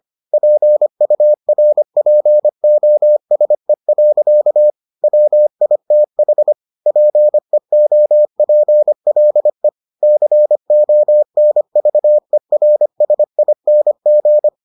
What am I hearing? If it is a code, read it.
Morse code, heard as a keyed tone, about 25 wpm, 600 Hz. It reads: PURPOSE. WITH PEOPLE CONVERSING